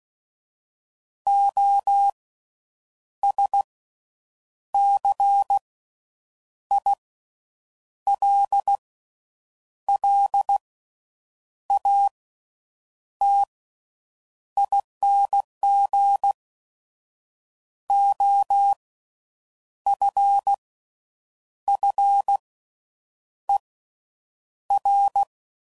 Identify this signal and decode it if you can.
Morse code, heard as a keyed tone, about 16 wpm, spaced as for 7 wpm, 800 Hz. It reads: OSCILLATING OFFER
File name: Slow With Faster Ings